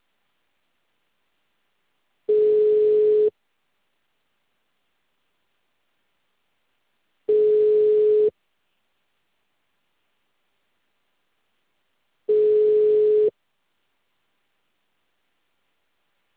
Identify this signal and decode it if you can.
Ringback tone (Europe, 425 Hz).
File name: out- -2004- - - .40533